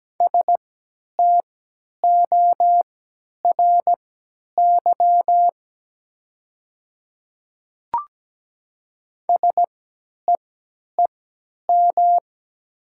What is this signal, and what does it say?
Morse code, heard as a keyed tone, about 17 wpm, spaced as for 10 wpm, 700 Hz. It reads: STORY SEEM
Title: 3x Character Spacing - Top 300 Words - 17wpm